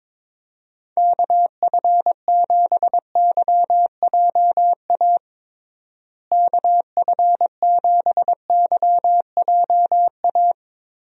Morse code, heard as a keyed tone, about 22 wpm, 700 Hz. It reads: KF7YJA KF7YJA
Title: USA Call Signs - 6 Characters - Repeated 3x - 22wpm